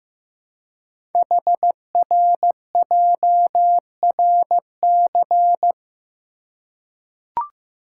Morse code, heard as a keyed tone, about 15 wpm, 700 Hz. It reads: HRJRC
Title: Code Groups - 5 Letters - 15wpm